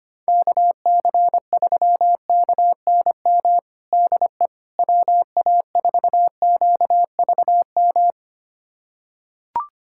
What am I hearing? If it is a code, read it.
Morse code, heard as a keyed tone, about 25 wpm, 700 Hz. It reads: KC3KNM DE WA4QVM
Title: USA Callsign DE Callsign - 6 Characters - 25wpm